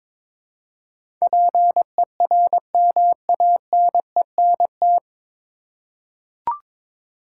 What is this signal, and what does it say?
Morse code, heard as a keyed tone, about 22 wpm, 700 Hz. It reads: PERMANENT
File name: 9 Letter Words - First 3 Letters Given - 22wpm